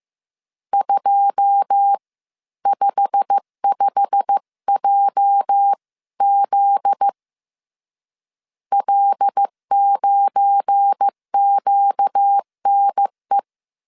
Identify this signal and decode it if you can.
Morse code, heard as a keyed tone, about 15 wpm, spaced as for 13 wpm, 800 Hz. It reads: 255JZ L9QNE
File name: CW / DET / 1 9 1500C S C 14wpm 3245